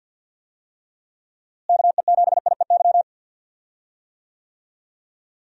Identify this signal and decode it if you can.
Morse code, heard as a keyed tone, about 46 wpm, 700 Hz. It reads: KE7IEY
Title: USA Call Signs - 6 Characters - Repeated 3x - 50wpm